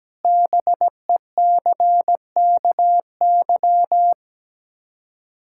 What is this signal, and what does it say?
Morse code, heard as a keyed tone, about 17 wpm, 700 Hz. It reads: BECKY
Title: K1USN SST QSO Practice 17wpm